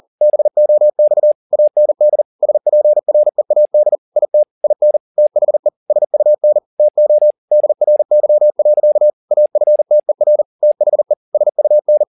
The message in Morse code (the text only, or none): BOX AND SPREAD IT IN THE SUN TO DRY. AFTER THE SUN